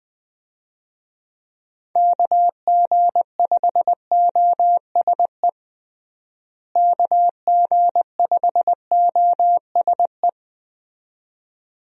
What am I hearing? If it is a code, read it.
Morse code, heard as a keyed tone, about 20 wpm, 700 Hz. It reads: KG5OSE KG5OSE
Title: USA Call Signs - 6 Characters - Repeated 2x - 20wpm